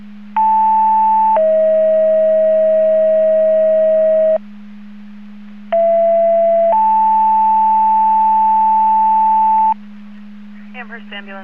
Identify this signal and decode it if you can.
Two-tone page, heard as a one-second tone, then a three-second tone, about 880 then 640 Hz, about 670 then 880 Hz. It is heard from 0.4 to 9.7 s.
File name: Portage / 00285 15-00 Portage 154.385 Paging